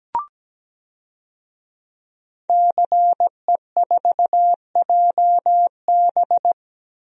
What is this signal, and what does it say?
Morse code, heard as a keyed tone, about 17 wpm, 700 Hz. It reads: CE4JB
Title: DXCC-100 Call Signs - 5 Characters - 17wpm